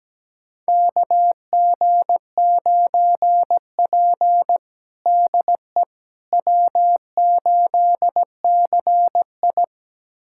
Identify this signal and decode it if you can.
Morse code, heard as a keyed tone, about 17 wpm, 700 Hz. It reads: KG9P DE W8CI
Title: USA Callsign DE Callsign - 4 Characters - 17wpm